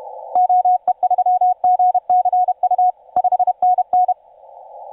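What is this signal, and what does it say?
Morse code, heard as a keyed tone, about 31 wpm, 730 Hz. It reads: OE3GCU 5NN
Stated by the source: TU7C on 20 metre CW, DX station recording by IZ8GNR.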